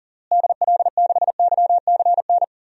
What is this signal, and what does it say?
Morse code, heard as a keyed tone, about 40 wpm, 700 Hz. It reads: DL6YCN
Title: DXCC-100 Call Signs - 6 Characters - Repeated 3x - 40wpm